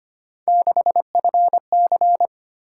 Morse code, heard as a keyed tone, about 25 wpm, 700 Hz. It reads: 6FC
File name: Three Letter-Number Combinations - 25wpm